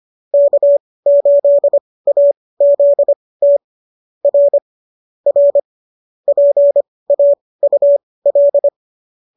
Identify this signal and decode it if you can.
Morse code, heard as a keyed tone, about 25 wpm, spaced as for 18 wpm, 590 Hz. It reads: K8AZT R R PAUL